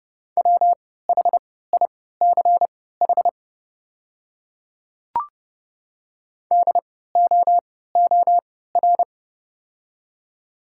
Morse code, heard as a keyed tone, about 30 wpm, spaced as for 17 wpm, 700 Hz. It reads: WHICH DOOR
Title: 3x Character Spacing - Top 300 Words - 30wpm